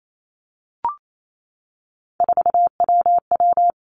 Morse code, heard as a keyed tone, about 28 wpm, 700 Hz. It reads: 4WW
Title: Three Challenging Letter-Number Combinations 28wpm